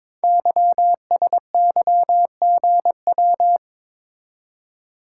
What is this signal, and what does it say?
Morse code, heard as a keyed tone, about 22 wpm, 700 Hz. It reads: YSYGW
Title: Code Groups - 5 Letters - 22wpm